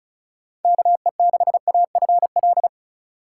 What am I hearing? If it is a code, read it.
Morse code, heard as a keyed tone, about 36 wpm, 700 Hz. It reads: KE6AFL
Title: USA Call Signs - 6 Characters - 2x Character Spacing - 35wpm